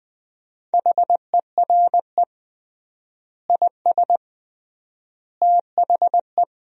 Morse code, heard as a keyed tone, about 20 wpm, 700 Hz. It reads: HERE IS THE
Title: 3x Word-Spacing - Sentences from Top 300 Words - 20wpm